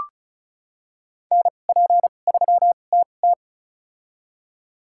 Morse code, heard as a keyed tone, about 36 wpm, spaced as for 25 wpm, 700 Hz. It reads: NP3TT
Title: USA Call Signs - 5 Characters - 2x Character Spacing - 35wpm